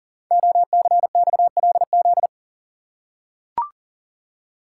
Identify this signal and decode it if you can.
Morse code, heard as a keyed tone, about 40 wpm, 700 Hz. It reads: OCXLZ
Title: Code Groups - 5 Letters - 40wpm